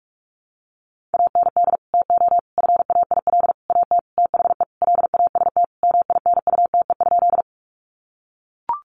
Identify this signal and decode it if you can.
Morse code, heard as a keyed tone, about 44 wpm, 700 Hz. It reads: AND TO FAIL AT THE LAST MINUTE?
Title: Sentences from Top 1100 Words 45wpm